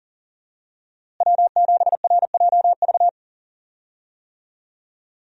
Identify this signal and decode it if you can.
Morse code, heard as a keyed tone, about 40 wpm, 700 Hz. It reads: W7RJV
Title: USA Call Signs - 5 Characters - Repeated 3x - 40wpm